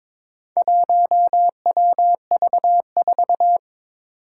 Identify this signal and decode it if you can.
Morse code, heard as a keyed tone, about 22 wpm, 700 Hz. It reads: 1WV4
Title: Four Challenging Letter-Number Combinations 22wpm